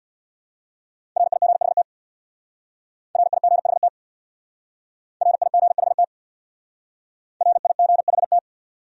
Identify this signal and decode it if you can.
Morse code, heard as a keyed tone, about 64 wpm, 700 Hz. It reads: RIGHT RIGHT RIGHT RIGHT